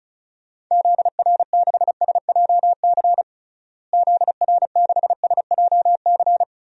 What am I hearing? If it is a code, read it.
Morse code, heard as a keyed tone, about 36 wpm, 700 Hz. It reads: ZR6SJC ZR6SJC
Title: DXCC-100 Call Signs - 6 Characters - Repeated 3x - 35wpm